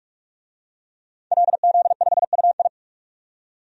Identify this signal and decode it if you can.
Morse code, heard as a keyed tone, about 46 wpm, 700 Hz. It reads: LZ5UI
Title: DXCC-100 Call Signs - 5 Characters - 45wpm